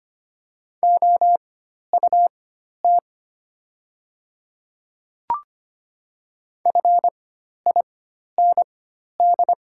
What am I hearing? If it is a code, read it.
Morse code, heard as a keyed tone, about 25 wpm, spaced as for 12 wpm, 700 Hz. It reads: OUT FIND
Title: 4x Character Spacing - Top 100 Words - 25wpm